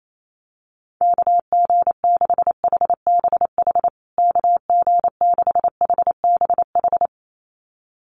Morse code, heard as a keyed tone, about 28 wpm, 700 Hz. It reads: KG6HBH KG6HBH
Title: USA Call Signs Repeated 2x 28wpm